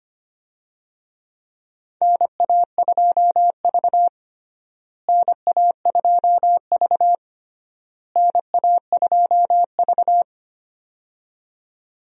Morse code, heard as a keyed tone, about 25 wpm, 700 Hz. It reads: NA2V NA2V NA2V
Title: USA Call Signs - 4 Characters - Repeated 3x - 25wpm